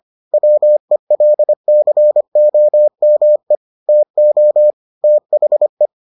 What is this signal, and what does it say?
Morse code, heard as a keyed tone, about 25 wpm, 600 Hz. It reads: WELCOME TO THE